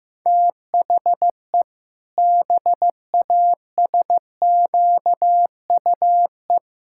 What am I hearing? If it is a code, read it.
Morse code, heard as a keyed tone, about 15 wpm, 700 Hz. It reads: THE BASQUE
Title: Pangram Challenge v2 - 15wpm